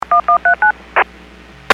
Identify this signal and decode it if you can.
DTMF, heard as telephone touch tones, tones of about 110 ms. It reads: 1139